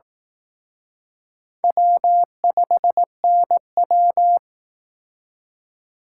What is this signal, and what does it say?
Morse code, heard as a keyed tone, about 18 wpm, 700 Hz. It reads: W5NW
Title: Speed Racing - USA Call Signs 15wpm